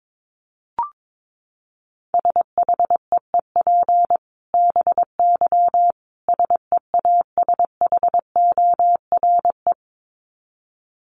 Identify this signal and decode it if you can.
Morse code, heard as a keyed tone, about 22 wpm, 700 Hz. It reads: SHEEP BY SEASHORE
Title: Dit Challenge 22wpm